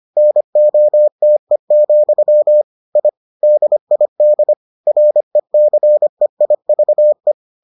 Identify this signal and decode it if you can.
Morse code, heard as a keyed tone, about 25 wpm, 600 Hz. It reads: NOTE, I DID RECEIVE